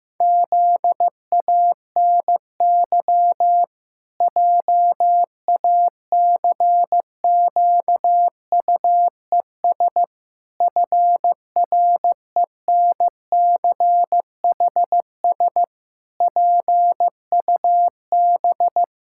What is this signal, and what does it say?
Morse code, heard as a keyed tone, about 15 wpm, 700 Hz. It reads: ZANY JACQUES FRENCHS PUB
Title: Pangram Challenge v2 - 15wpm